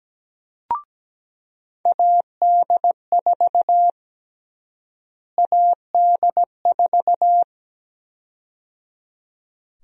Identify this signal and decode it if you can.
Morse code, heard as a keyed tone, about 17 wpm, 700 Hz. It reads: AD4 AD4
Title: USA Call Sign Prefixes - Repeated 2x - 17wpm